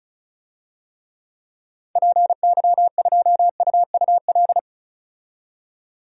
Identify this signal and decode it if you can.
Morse code, heard as a keyed tone, about 34 wpm, 700 Hz. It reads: PY2UUL
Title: DXCC-100 Call Signs - 6 Characters - 35wpm